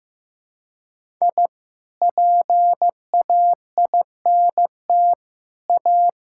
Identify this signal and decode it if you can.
Morse code, heard as a keyed tone, about 14 wpm, 700 Hz. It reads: I PAINT A
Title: Sentences from Top 1000 Words - Review Part 6 - 15wpm